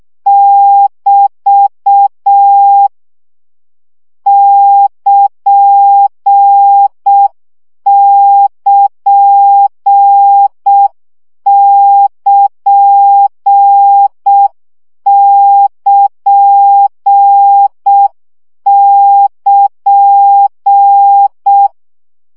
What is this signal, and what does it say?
Morse code, heard as a keyed tone, about 6 wpm, 800 Hz. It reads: <BT> <KN><KN><KN><KN><KN>
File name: CW gyakorlas kn 5 baud